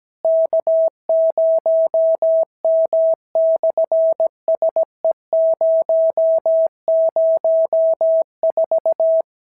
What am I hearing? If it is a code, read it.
Morse code, heard as a keyed tone, about 17 wpm, 650 Hz. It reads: K0M/SE004
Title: SOTA QSO Practice 17wpm